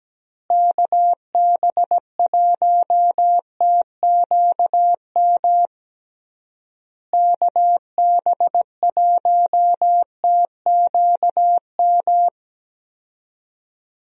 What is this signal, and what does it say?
Morse code, heard as a keyed tone, about 17 wpm, 700 Hz. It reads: KB1TQM KB1TQM
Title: USA Call Signs - 6 Characters - Repeated 3x - 17wpm